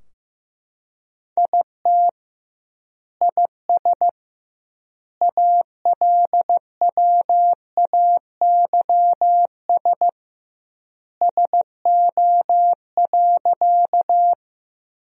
Morse code, heard as a keyed tone, about 15 wpm, 700 Hz. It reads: IT IS ALWAYS SO.
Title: 2x Word-Spacing - Sentences from Top 300 Words - 15wpm